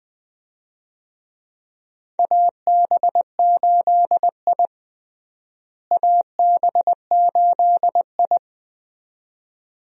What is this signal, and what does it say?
Morse code, heard as a keyed tone, about 20 wpm, 700 Hz. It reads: AB8I AB8I
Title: USA Call Signs - 4 Characters - Repeated 2x - 20wpm